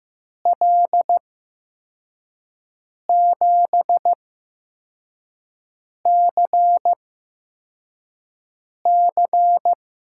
Morse code, heard as a keyed tone, about 15 wpm, spaced as for 4 wpm, 700 Hz. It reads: L7CC
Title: USA Call Signs - 8x Character Spacing - 15wpm